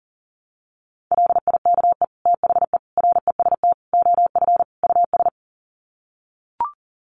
Morse code, heard as a keyed tone, about 40 wpm, 700 Hz. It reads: LIKE THE REST OF US
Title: Sets of 5 Words Encore2 40wpm